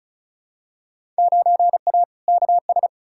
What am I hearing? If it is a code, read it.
Morse code, heard as a keyed tone, about 36 wpm, 700 Hz. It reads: 9A KS
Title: Field Day Exchanges Encore - 35wpm